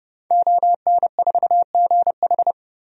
Morse code, heard as a keyed tone, about 30 wpm, 700 Hz. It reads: ON4GH
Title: DXCC-100 Call Signs - 5 Characters - 30wpm